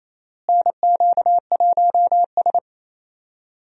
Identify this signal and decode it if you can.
Morse code, heard as a keyed tone, about 28 wpm, 700 Hz. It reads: NQ1S
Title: USA Call Signs - 4 Characters - Repeated 2x - 28wpm